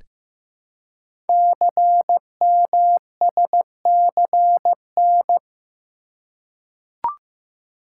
Morse code, heard as a keyed tone, about 15 wpm, 700 Hz. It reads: CMSCN
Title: Code Groups - 5 Letters - 15wpm